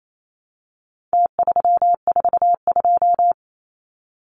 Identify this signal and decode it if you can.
Morse code, heard as a keyed tone, about 28 wpm, 700 Hz. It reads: T342
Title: Numbers - Contest Style - 4-Digit 28wpm